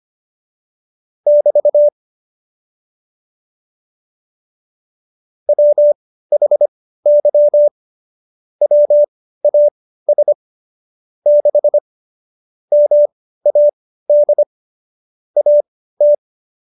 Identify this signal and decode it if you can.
Morse code, heard as a keyed tone, about 25 wpm, spaced as for 15 wpm, 590 Hz. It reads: <BT> WHY WAS 6 MAD AT
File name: Jokes-25-char-15-eff